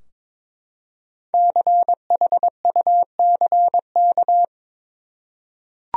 Morse code, heard as a keyed tone, about 22 wpm, 700 Hz. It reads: CHUCK
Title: USA Common Names - Max 5 Letters - 22wpm